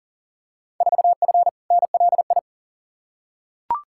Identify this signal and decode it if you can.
Morse code, heard as a keyed tone, about 40 wpm, 700 Hz. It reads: 4F NLI
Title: Field Day Exchanges Encore - 40wpm